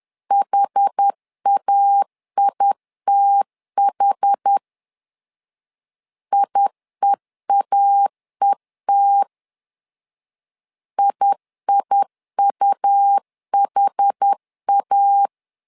Morse code, heard as a keyed tone, about 10 wpm, 800 Hz. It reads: HAITH IEAET IIUHA
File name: LEZIONI / 2 UHDN / 2 UH 10wpm 1714